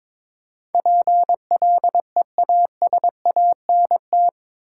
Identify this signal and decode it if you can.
Morse code, heard as a keyed tone, about 22 wpm, 700 Hz. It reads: PLEASANT
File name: practice / 22wpm / Eight Letter Words - 22wpm